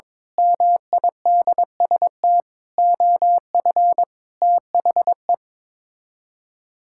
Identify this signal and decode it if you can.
Morse code, heard as a keyed tone, about 22 wpm, 700 Hz. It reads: MIDST OF THE
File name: practice / 22wpm / Build A Phrase - 5-Word Phrases - 22wpm